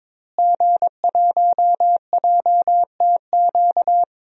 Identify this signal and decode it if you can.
Morse code, heard as a keyed tone, about 22 wpm, 700 Hz. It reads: G1JTQ
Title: DXCC-100 Call Signs - 5 Characters - Repeated 3x - 22wpm